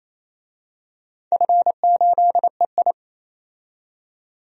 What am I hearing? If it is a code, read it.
Morse code, heard as a keyed tone, about 28 wpm, 700 Hz. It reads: F8EI
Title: DXCC-100 Call Signs - 4 Characters - 28wpm